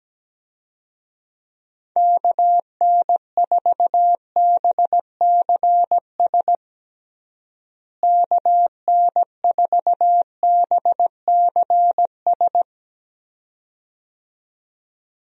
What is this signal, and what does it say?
Morse code, heard as a keyed tone, about 17 wpm, 700 Hz. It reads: KN4BCS KN4BCS
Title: USA Call Signs - 6 Characters - Repeated 2x - 17wpm